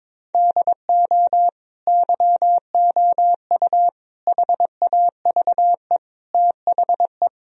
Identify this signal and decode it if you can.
Morse code, heard as a keyed tone, about 22 wpm, 700 Hz. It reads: DO YOU HAVE THE
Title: Sentences from Top 1000 Words - Review Part 4 - 22wpm